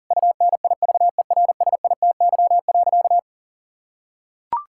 Morse code, heard as a keyed tone, about 40 wpm, 700 Hz. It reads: UNIVERSITY.
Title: Sentences from Top 1000 Words - Review Part 5 - 40wpm